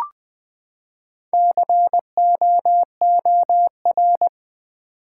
Morse code, heard as a keyed tone, about 20 wpm, 700 Hz. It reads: COOR
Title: Build A Word - 4 Letters At A Time - 12-Letter Words - 20wpm